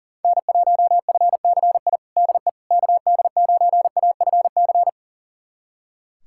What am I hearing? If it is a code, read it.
Morse code, heard as a keyed tone, about 40 wpm, 700 Hz. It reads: N1FCI DE KD9AFC
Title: USA Callsign DE Callsign 40wpm